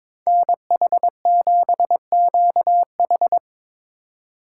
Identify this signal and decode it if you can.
Morse code, heard as a keyed tone, about 25 wpm, 700 Hz. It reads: NH7QH
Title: Speed Racing - USA 2x2 Call Signs 22wpm